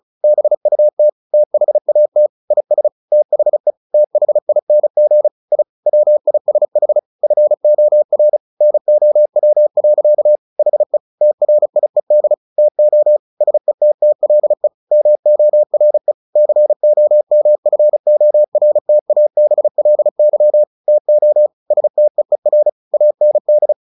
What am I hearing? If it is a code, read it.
Morse code, heard as a keyed tone, about 36 wpm, 600 Hz. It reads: BUT THAT IS THE THING I WISH FOR NOW. HE TRIED TO SETTLE MORE COMFORTABLY TO STEER AND
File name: the old man and the sea 2 35wpm